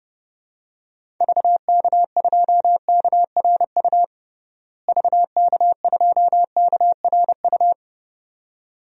Morse code, heard as a keyed tone, about 32 wpm, 700 Hz. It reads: VK2KRU VK2KRU
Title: DXCC-100 Call Signs - 6 Characters - Repeated 2x - 30wpm